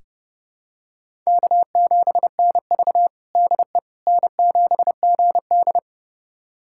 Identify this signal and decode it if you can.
Morse code, heard as a keyed tone, about 30 wpm, 700 Hz. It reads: K7NV DE N7GD
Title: USA Callsign DE Callsign - 4 Characters - 30wpm